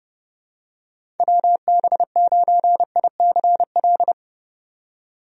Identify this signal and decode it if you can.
Morse code, heard as a keyed tone, about 32 wpm, 700 Hz. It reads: WB9ICL
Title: USA Call Signs - 6 Characters - 30wpm